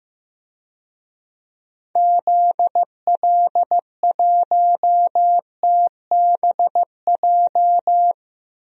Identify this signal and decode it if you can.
Morse code, heard as a keyed tone, about 15 wpm, 700 Hz. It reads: ZL1TBJ
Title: DXCC-100 Call Signs - 6 Characters - Repeated 2x - 15wpm